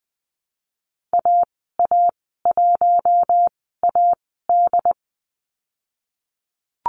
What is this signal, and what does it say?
Morse code, heard as a keyed tone, about 20 wpm, spaced as for 15 wpm, 700 Hz. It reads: AA1AD
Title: 088 Lesson 15g - Number 5 - Call Signs 20wpm